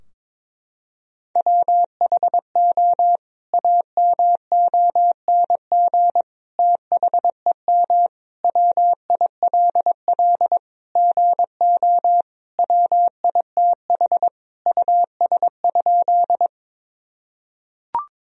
Morse code, heard as a keyed tone, about 22 wpm, 700 Hz. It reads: WHO AMONG THEM WILL GO WITH US?
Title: Sentences from Top 1000 Words - Review Part 3 - 22wpm